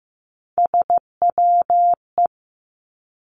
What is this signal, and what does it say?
Morse code, heard as a keyed tone, about 15 wpm, 700 Hz. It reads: SWE
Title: Three Challenging Letter-Number Combinations 15wpm